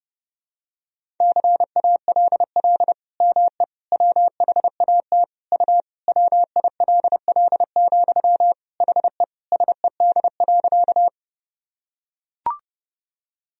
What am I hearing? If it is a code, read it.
Morse code, heard as a keyed tone, about 30 wpm, 700 Hz. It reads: CALL ME WHAT U WILL, HE SED.
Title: Sentences from Top 200 Words and Abbreviations 30wpm